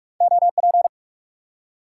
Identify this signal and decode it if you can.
Morse code, heard as a keyed tone, about 45 wpm, 700 Hz. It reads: OP
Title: Top 90 QSO Elements 45wpm